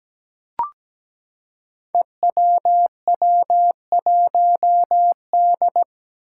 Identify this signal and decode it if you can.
Morse code, heard as a keyed tone, about 17 wpm, 700 Hz. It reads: EWW1D
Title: Five Challenging Letter-Number Combinations 17wpm